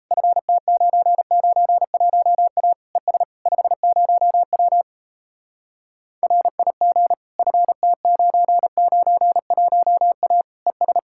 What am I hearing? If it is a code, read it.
Morse code, heard as a keyed tone, about 36 wpm, 700 Hz. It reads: FT991A ES 50W RIG FT991A ES